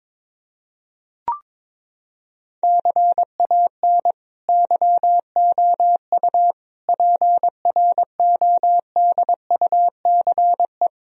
Morse code, heard as a keyed tone, about 22 wpm, 700 Hz. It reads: CAN YOU PRODUCE
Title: Sentences from Top 1000 Words - Review Part 3 - 22wpm